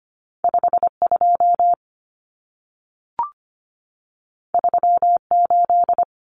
Morse code, heard as a keyed tone, about 25 wpm, 700 Hz. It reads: 52 38
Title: Numbers - 2-Digit 25wpm